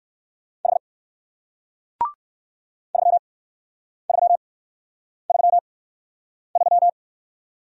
Morse code, heard as a keyed tone, about 55 wpm, spaced as for 9 wpm, 700 Hz. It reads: H 3333
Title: Speed-Racing Single Letter-Number 45wpm